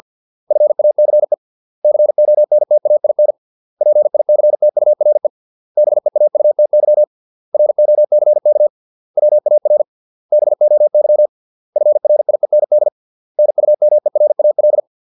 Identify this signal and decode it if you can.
Morse code, heard as a keyed tone, about 50 wpm, 600 Hz. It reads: FACE CONTAIN PICTURE BEAUTY ROCK WAR BOY FRIEND NUMERAL